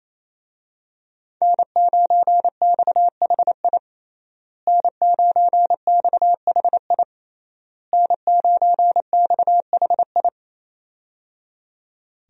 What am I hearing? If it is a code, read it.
Morse code, heard as a keyed tone, about 27 wpm, 700 Hz. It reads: N9XHI N9XHI N9XHI
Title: USA Call Signs - 5 Characters - Repeated 3x - 28wpm